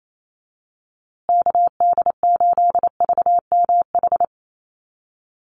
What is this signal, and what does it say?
Morse code, heard as a keyed tone, about 28 wpm, 700 Hz. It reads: KD8VMH
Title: USA Call Signs Repeated 3x 28wpm